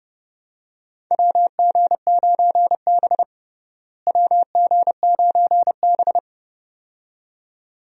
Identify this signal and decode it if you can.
Morse code, heard as a keyed tone, about 30 wpm, 700 Hz. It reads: WG9B WG9B
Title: USA Call Signs - 4 Characters - Repeated 2x - 30wpm